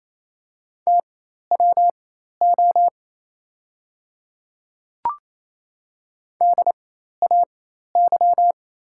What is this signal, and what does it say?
Morse code, heard as a keyed tone, about 28 wpm, spaced as for 13 wpm, 690 Hz. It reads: TWO DAY
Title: 4x Character Spacing - Top 100 Words - 28wpm